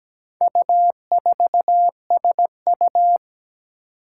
Morse code, heard as a keyed tone, about 17 wpm, 700 Hz. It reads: U4SU
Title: Four Challenging Letter-Number Combinations 17wpm